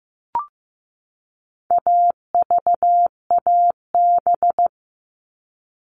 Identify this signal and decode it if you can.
Morse code, heard as a keyed tone, about 15 wpm, 700 Hz. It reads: AVAB
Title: All Cut Numbers - 4-Digit 15wpm